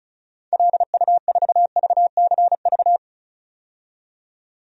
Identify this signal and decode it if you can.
Morse code, heard as a keyed tone, about 36 wpm, 700 Hz. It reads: LU4VCV
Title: DXCC-100 Call Signs - 6 Characters - 35wpm